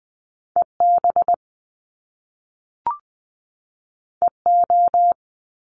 Morse code, heard as a keyed tone, about 20 wpm, 700 Hz. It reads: EB EO
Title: Top Two Letter Combinations in Words 20wpm